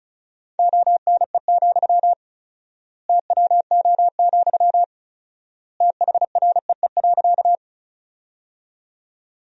Morse code, heard as a keyed tone, about 34 wpm, 700 Hz. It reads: ONE, TWO, THREE.
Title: 4x Word-Spacing - Sentences from Top 200 Words - 35wpm